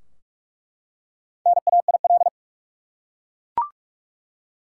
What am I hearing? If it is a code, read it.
Morse code, heard as a keyed tone, about 47 wpm, 700 Hz. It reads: NAIL
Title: Four Letter Words - 45wpm